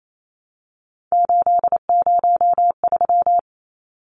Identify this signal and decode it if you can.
Morse code, heard as a keyed tone, about 28 wpm, 700 Hz. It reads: 803
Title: Numbers - 3-Digit 28wpm